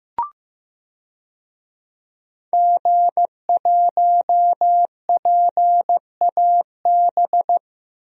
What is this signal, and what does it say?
Morse code, heard as a keyed tone, about 15 wpm, 700 Hz. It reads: G1PAB
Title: DXCC-100 Call Signs - 5 Characters - 15wpm